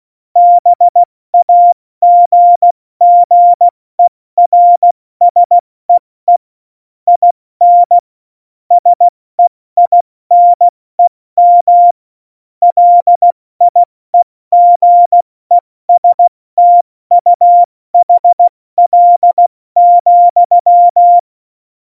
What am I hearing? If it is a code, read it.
Morse code, heard as a keyed tone, about 16 wpm, 700 Hz. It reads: BAGGERSEE IN SEINEM LIEGESTUHL,
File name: Morsekurs / Witze / Witz 08. Dezember 2020, 70 F80